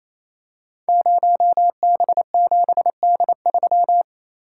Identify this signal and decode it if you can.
Morse code, heard as a keyed tone, about 28 wpm, 700 Hz. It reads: 0B7D3
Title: Code Groups - 5 Letter:Numbers - 28wpm